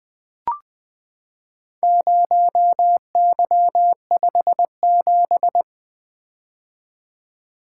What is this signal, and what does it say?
Morse code, heard as a keyed tone, about 20 wpm, 700 Hz. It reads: 0Y57
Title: Four Letter-Number Combinations - 20wpm